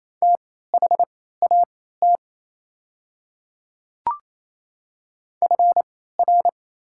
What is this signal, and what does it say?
Morse code, heard as a keyed tone, about 27 wpm, spaced as for 16 wpm, 700 Hz. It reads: THAT FR
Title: 3x Character Spacing - Top 200 Words - 28wpm